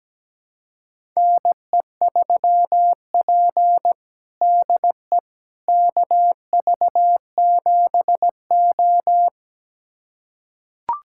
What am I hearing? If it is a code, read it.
Morse code, heard as a keyed tone, about 17 wpm, 700 Hz. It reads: NE3P DE KV7O
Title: USA Callsign DE Callsign - 4 Characters - 17wpm